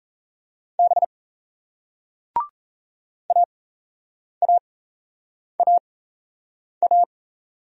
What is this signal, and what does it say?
Morse code, heard as a keyed tone, about 38 wpm, spaced as for 8 wpm, 700 Hz. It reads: B AAAA